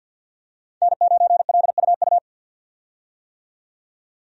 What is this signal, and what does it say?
Morse code, heard as a keyed tone, about 47 wpm, 700 Hz. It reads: N9LUU
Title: USA Call Signs - 5 Characters - 50wpm